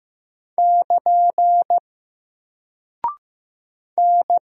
Morse code, heard as a keyed tone, about 15 wpm, 700 Hz. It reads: <KN> N
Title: Single Character (Letter:Number:Punctuation:Prosign) Alt 15wpm